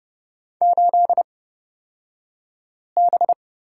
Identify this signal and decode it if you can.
Morse code, heard as a keyed tone, about 30 wpm, spaced as for 5 wpm, 700 Hz. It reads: 8B
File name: Lesson 38d - Letter q - Two Characters 30wpm